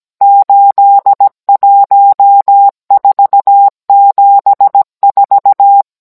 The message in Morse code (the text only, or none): 81474